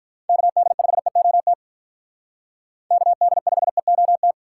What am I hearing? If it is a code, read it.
Morse code, heard as a keyed tone, about 45 wpm, 700 Hz. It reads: KD5EOT KD5EOT